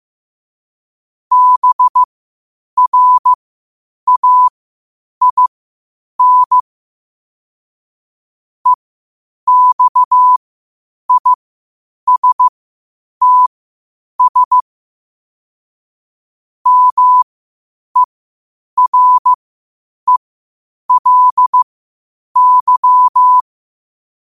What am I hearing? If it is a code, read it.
Morse code, heard as a keyed tone, about 15 wpm, spaced as for 8 wpm, 1000 Hz. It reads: BRAIN EXISTS MERELY
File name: qotdfiles / 8wpm